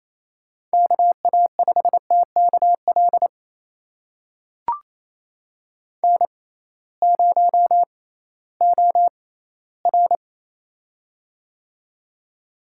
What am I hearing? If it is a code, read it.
Morse code, heard as a keyed tone, about 28 wpm, spaced as for 16 wpm, 700 Hz. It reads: KA5TKL N0OR
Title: USA Call Signs - 6x Character Spacing - 28wpm